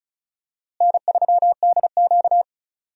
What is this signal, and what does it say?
Morse code, heard as a keyed tone, about 36 wpm, 700 Hz. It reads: N3DQ
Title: USA Call Signs - 4 Characters - Repeated 2x - 35wpm